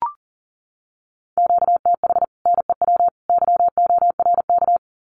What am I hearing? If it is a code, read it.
Morse code, heard as a keyed tone, about 40 wpm, 700 Hz. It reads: QTH NEW YORK
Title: Common QSO Phrases 40wpm